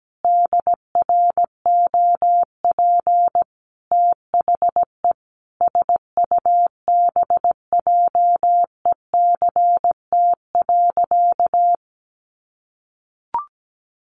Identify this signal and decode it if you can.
Morse code, heard as a keyed tone, about 17 wpm, 700 Hz. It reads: DROP THE SUBJECT.
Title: Sentences from Top 1400 Words 17wpm